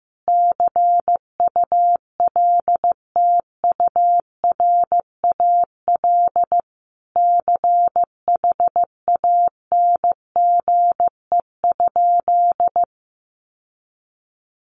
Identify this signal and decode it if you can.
Morse code, heard as a keyed tone, about 15 wpm, 700 Hz. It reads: CULTURAL CHANGE?
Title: Sentences from Top 1500 Words 15wpm